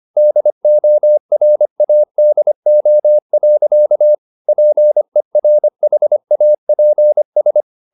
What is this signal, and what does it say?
Morse code, heard as a keyed tone, about 25 wpm, 600 Hz. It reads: DORADO. PERHAPS